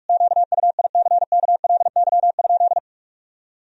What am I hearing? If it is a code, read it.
Morse code, heard as a keyed tone, about 46 wpm, 700 Hz. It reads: QUICKLY?
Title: Sentences from Top 1000 Words - Review Part 3 - 45wpm